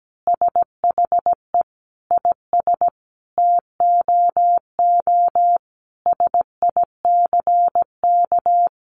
Morse code, heard as a keyed tone, about 17 wpm, 700 Hz. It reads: SHE IS TOO SICK
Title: Easy Phrases 17wpm